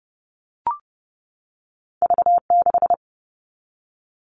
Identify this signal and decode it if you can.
Morse code, heard as a keyed tone, about 31 wpm, 700 Hz. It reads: V6
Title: All Cut Numbers - 2-Digit 30wpm